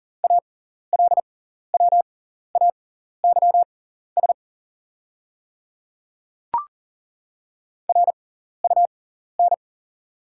Morse code, heard as a keyed tone, about 40 wpm, spaced as for 14 wpm, 690 Hz. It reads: ALWAYS RUN
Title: 6x Character Spacing - Top 500 Words - 40wpm